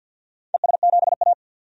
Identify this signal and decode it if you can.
Morse code, heard as a keyed tone, about 51 wpm, 700 Hz. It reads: ES7A
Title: DXCC-100 Call Signs - 4 Characters - 50wpm